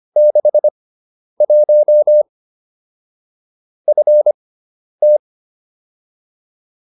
Morse code, heard as a keyed tone, about 25 wpm, spaced as for 10 wpm, 600 Hz. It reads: 61 FT
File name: QSO310 10